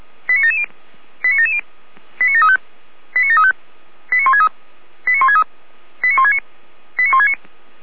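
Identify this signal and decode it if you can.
Five-tone selective call (ZVEI), heard as quick runs of five tones, tones of about 80 ms. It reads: 89770 89770 89735 89735 89163 89163 89169 89169
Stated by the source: ZVEI five-tone selective call received on an SDR.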